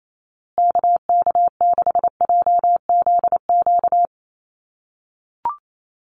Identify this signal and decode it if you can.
Morse code, heard as a keyed tone, about 28 wpm, 700 Hz. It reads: KK6JZQ
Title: USA Call Signs Repeated 2x 28wpm